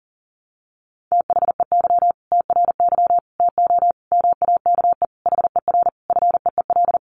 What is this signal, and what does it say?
Morse code, heard as a keyed tone, about 40 wpm, 700 Hz. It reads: THEY TRY TO MAKE HER FEEL
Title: Easy Phrases 40wpm